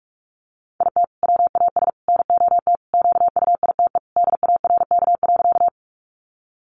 Morse code, heard as a keyed tone, about 44 wpm, 700 Hz. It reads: IT WAS NOT QUITE DARK.
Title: Sentences from Top 700 Words 45wpm